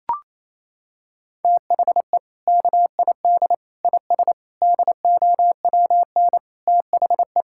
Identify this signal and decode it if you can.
Morse code, heard as a keyed tone, about 28 wpm, 700 Hz. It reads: THE KID IS DOWN THE
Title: Sentences from Top 1600 Words 28wpm